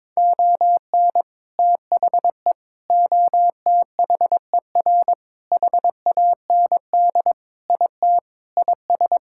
Morse code, heard as a keyed tone, about 22 wpm, 700 Hz. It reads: ON THE OTHER HAND IT IS
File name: practice / 22wpm / Sentences with 4-Word Phrases - 22wpm